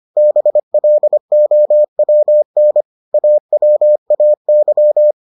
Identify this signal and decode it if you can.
Morse code, heard as a keyed tone, about 25 wpm, 600 Hz. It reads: BLOWN AWAY